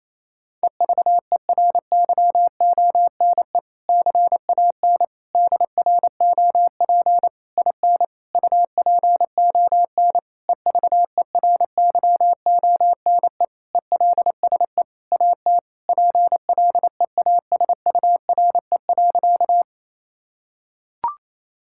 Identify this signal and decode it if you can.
Morse code, heard as a keyed tone, about 28 wpm, 700 Hz. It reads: EVERYONE CAN DROP IN UPON EVERYONE ELSE AT PLEASURE.
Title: Sentences from Top 1600 Words 28wpm